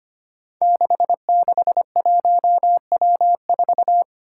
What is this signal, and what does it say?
Morse code, heard as a keyed tone, about 25 wpm, 700 Hz. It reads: 661W4
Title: Five Challenging Letter-Number Combinations 25wpm